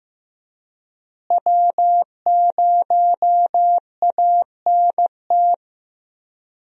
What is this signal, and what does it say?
Morse code, heard as a keyed tone, about 15 wpm, 700 Hz. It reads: W0ANT
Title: USA Call Signs - 5 Characters - Repeated 2x - 15wpm